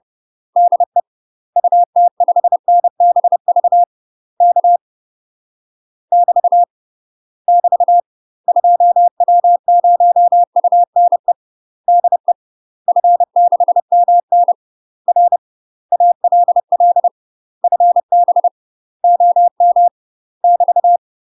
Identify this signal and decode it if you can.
Morse code, heard as a keyed tone, about 30 wpm, 700 Hz. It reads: DE UT5NBV K <BT> <BT> 2W0UNE DE F6MN R ALL FB OM <BT>